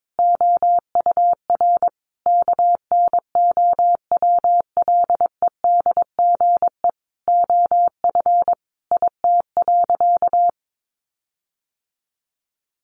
Morse code, heard as a keyed tone, about 22 wpm, 700 Hz. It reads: OUR KNOWLEDGE OF IT.
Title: Sentences from Top 1400 Words 22wpm